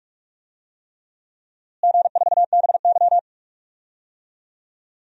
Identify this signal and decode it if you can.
Morse code, heard as a keyed tone, about 43 wpm, 700 Hz. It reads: G4BY
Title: DXCC-100 Call Signs - 4 Characters - Repeated 3x - 45wpm